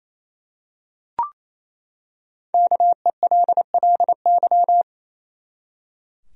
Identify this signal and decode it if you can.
Morse code, heard as a keyed tone, about 28 wpm, 700 Hz. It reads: KELLY
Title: USA Common Names - 28wpm